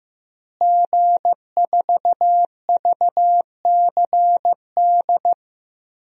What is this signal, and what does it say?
Morse code, heard as a keyed tone, about 15 wpm, 700 Hz. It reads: G4VCD
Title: DXCC-100 Call Signs - 5 Characters - Repeated 2x - 15wpm